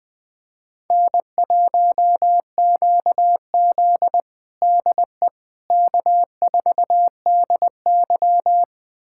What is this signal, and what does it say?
Morse code, heard as a keyed tone, about 20 wpm, 700 Hz. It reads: N1QZ DE K4DY
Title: USA Callsign DE Callsign - 4 Characters - 20wpm